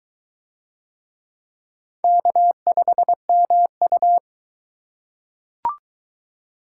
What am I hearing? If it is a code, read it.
Morse code, heard as a keyed tone, about 22 wpm, 700 Hz. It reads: K5MU